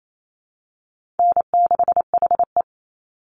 Morse code, heard as a keyed tone, about 28 wpm, 700 Hz. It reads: N6HE
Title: Easy Call Signs Repeated 2x - 28wpm